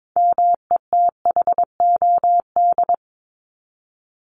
Morse code, heard as a keyed tone, about 22 wpm, 700 Hz. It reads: METHOD
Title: Top 501-750 Words 22wpm